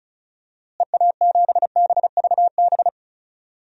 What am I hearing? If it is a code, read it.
Morse code, heard as a keyed tone, about 36 wpm, 700 Hz. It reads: EA7BVB
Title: DXCC-100 Call Signs - 6 Characters - Repeated 2x - 35wpm